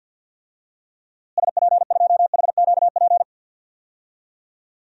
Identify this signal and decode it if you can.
Morse code, heard as a keyed tone, about 49 wpm, 700 Hz. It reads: SP2HQP
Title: DXCC-100 Call Signs - 6 Characters - Repeated 3x - 50wpm